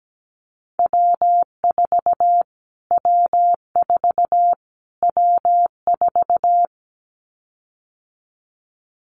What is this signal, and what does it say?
Morse code, heard as a keyed tone, about 17 wpm, 700 Hz. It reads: W4 W4 W4
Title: Callsign Prefixes for Top 25 DXCC - Repeated 3x - 17wpm